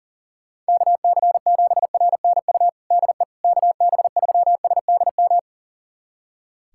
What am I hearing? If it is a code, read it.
Morse code, heard as a keyed tone, about 37 wpm, 700 Hz. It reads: KC7RNU DE KB3SDM
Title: USA Callsign DE Callsign - 6 Characters - 40wpm